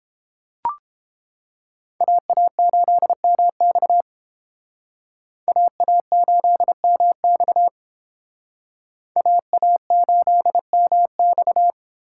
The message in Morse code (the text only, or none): AA8MX AA8MX AA8MX